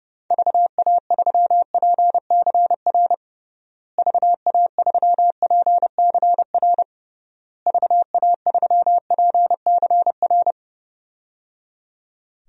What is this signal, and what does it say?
Morse code, heard as a keyed tone, about 30 wpm, 700 Hz. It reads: VA3PCR VA3PCR VA3PCR
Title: DXCC-100 Call Signs - 6 Characters - Repeated 3x - 30wpm